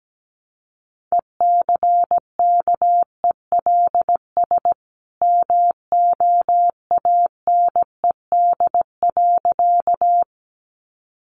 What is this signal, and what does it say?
Morse code, heard as a keyed tone, about 17 wpm, 700 Hz. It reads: ECKELS MOANED.
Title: A Sound Of Thunder 17wpm